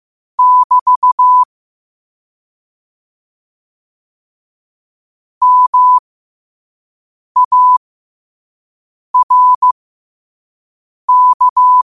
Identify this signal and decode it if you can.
Morse code, heard as a keyed tone, about 15 wpm, spaced as for 5 wpm, 1000 Hz. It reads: <BT> MARK
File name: qotdfiles / 5wpm